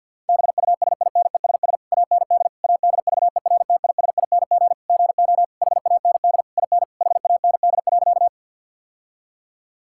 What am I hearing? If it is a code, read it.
Morse code, heard as a keyed tone, about 51 wpm, 700 Hz. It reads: BUSINESS AND ADVERTISING GO HAND IN HAND.
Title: Sentences from Top 1000 Words - Review Part 6 - 50wpm